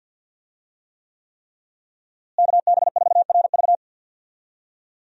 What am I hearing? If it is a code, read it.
Morse code, heard as a keyed tone, about 52 wpm, 700 Hz. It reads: KB4RV
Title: USA Call Signs - 5 Characters - Repeated 3x - 50wpm